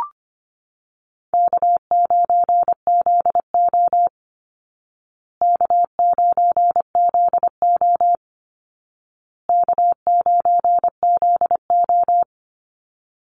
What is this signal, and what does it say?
Morse code, heard as a keyed tone, about 25 wpm, 700 Hz. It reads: K9ZO K9ZO K9ZO
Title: Easy Call Signs Repeated 3x - 25wpm